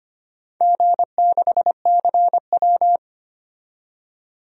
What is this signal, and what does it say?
Morse code, heard as a keyed tone, about 25 wpm, 690 Hz. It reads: G6CW
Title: DXCC-100 Call Signs - 4 Characters - 25wpm